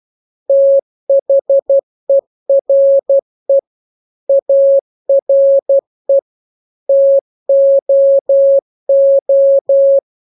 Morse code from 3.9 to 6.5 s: ARE